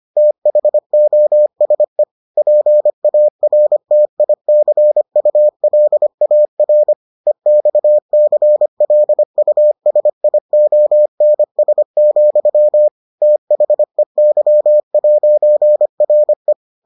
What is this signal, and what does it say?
Morse code, heard as a keyed tone, about 25 wpm, 600 Hz. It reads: THOSE PARTICULAR EXCLUSIONS, THEY'RE